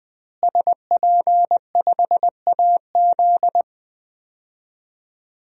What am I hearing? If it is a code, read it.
Morse code, heard as a keyed tone, about 20 wpm, 700 Hz. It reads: SP5AZ
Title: DXCC-100 Call Signs - 5 Characters - Repeated 3x - 20wpm